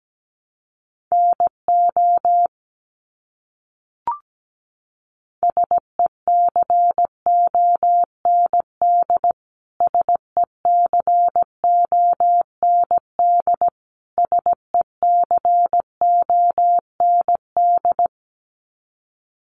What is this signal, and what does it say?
Morse code, heard as a keyed tone, about 17 wpm, 700 Hz. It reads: NO SECOND SECOND SECOND
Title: Top 500 Words Repeated 17wpm